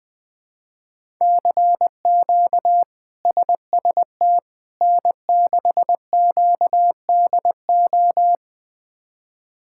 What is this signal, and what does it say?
Morse code, heard as a keyed tone, about 20 wpm, 700 Hz. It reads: CQ SST N6QDO
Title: K1USN SST QSO Practice - 20wpm